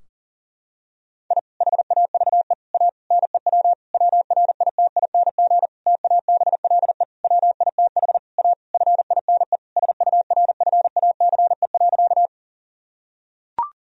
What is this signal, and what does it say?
Morse code, heard as a keyed tone, about 40 wpm, 700 Hz. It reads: I HAVE A NEW WRITING TABLE WITH A FINE SURFACE.
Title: Sentences from Top 1000 Words - Review Part 3 - 40wpm